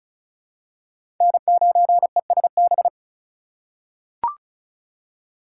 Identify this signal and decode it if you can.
Morse code, heard as a keyed tone, about 34 wpm, 700 Hz. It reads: N9ESB
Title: USA Call Signs - 5 Characters - 3x Character Spacing - 35wpm